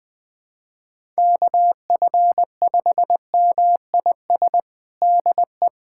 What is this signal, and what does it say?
Morse code, heard as a keyed tone, about 20 wpm, 700 Hz. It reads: KF5MIS DE
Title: USA Callsign DE Callsign - 6 Characters - 20wpm